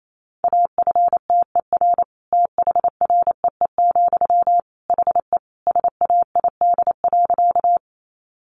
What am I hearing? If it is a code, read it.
Morse code, heard as a keyed tone, about 28 wpm, 700 Hz. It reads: AFTER THREE, HE SAID.
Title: Sentences from Top 200 Words 28wpm